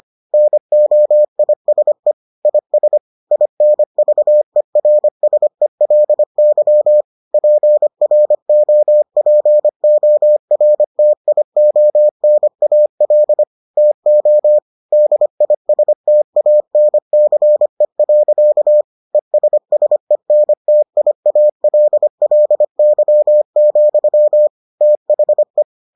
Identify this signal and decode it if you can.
Morse code, heard as a keyed tone, about 25 wpm, 600 Hz. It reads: NOISE IS INVERSELY PROPORTIONAL TO DISTANCE. ESSENTIALLY, THE